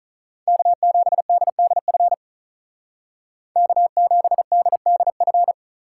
Morse code, heard as a keyed tone, about 46 wpm, 700 Hz. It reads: K7DDF K7DDF